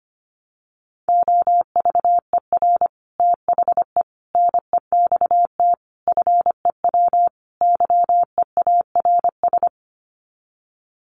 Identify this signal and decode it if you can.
Morse code, heard as a keyed tone, about 25 wpm, 700 Hz. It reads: OVER THE NEXT FEW YEARS
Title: Sets of 5 Words Encore 25wpm